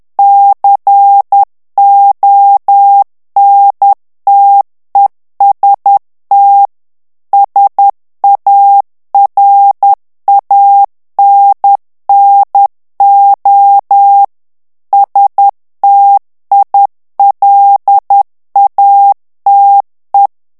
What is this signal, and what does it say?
Morse code, heard as a keyed tone, about 10 wpm, 800 Hz. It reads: CONTEST SARANNO STILATE
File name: ITALIEN texte clair 2 vitesse 10 mots minute